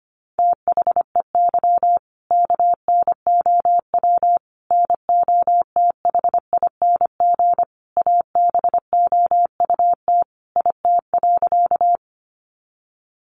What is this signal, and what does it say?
Morse code, heard as a keyed tone, about 25 wpm, 700 Hz. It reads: THEY KNOW NOTHING ABOUT IT.
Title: Sentences from Top 500 Words 25wpm